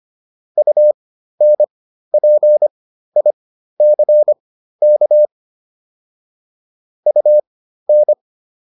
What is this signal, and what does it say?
Morse code, heard as a keyed tone, about 25 wpm, spaced as for 13 wpm, 600 Hz. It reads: UNPICK UN